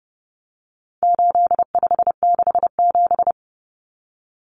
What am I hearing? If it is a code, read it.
Morse code, heard as a keyed tone, about 30 wpm, 700 Hz. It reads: 8567
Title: Numbers - 4-Digit 30wpm